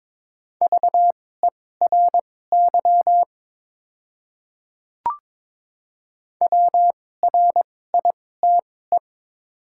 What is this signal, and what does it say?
Morse code, heard as a keyed tone, about 22 wpm, spaced as for 16 wpm, 690 Hz. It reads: VERY WRITE
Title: 2x Character Spacing - Top 200 Words - 22wpm